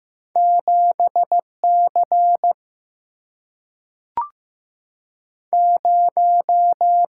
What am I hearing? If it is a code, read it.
Morse code, heard as a keyed tone, about 15 wpm, 700 Hz. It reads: 7C 0
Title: Single-Letter Single-Number 15wpm